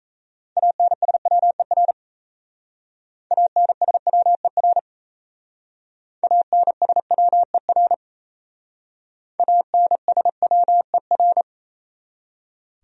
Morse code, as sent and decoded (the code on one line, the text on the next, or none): .- -. ... .-- . .-. / .- -. ... .-- . .-. / .- -. ... .-- . .-. / .- -. ... .-- . .-.
ANSWER ANSWER ANSWER ANSWER